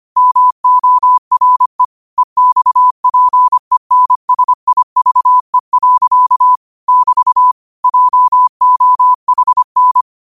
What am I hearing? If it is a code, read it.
Morse code, heard as a keyed tone, about 25 wpm, 1000 Hz. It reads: MORE EXPENSIVE. <BT> JOHN
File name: qotdfiles / 25wpm